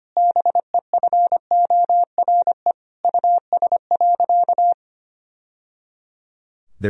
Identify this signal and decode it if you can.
Morse code, heard as a keyed tone, about 25 wpm, 700 Hz. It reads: BEFORE US.
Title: Sentences from Top 1000 Words - Review Part 4 - 25wpm